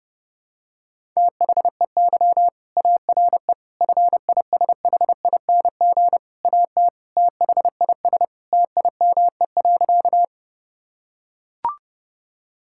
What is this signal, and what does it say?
Morse code, heard as a keyed tone, about 32 wpm, 700 Hz. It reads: THEY ARE FISHING AT THIS TIME.
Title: Sentences from Top 1000 Words - Review Part 5 - 30wpm